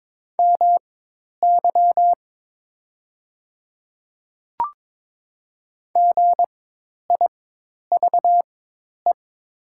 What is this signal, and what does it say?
Morse code, heard as a keyed tone, about 22 wpm, spaced as for 10 wpm, 700 Hz. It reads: MY GIVE
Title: 4x Character Spacing - Top 500 Words - 22wpm